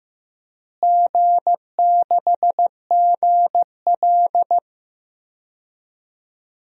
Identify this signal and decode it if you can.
Morse code, heard as a keyed tone, about 15 wpm, 700 Hz. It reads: G6GL
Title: DXCC-100 Call Signs - 4 Characters - Repeated 2x - 15wpm